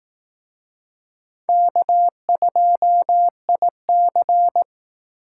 Morse code, heard as a keyed tone, about 18 wpm, 700 Hz. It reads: K2IC
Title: Speed Racing - USA Call Signs 15wpm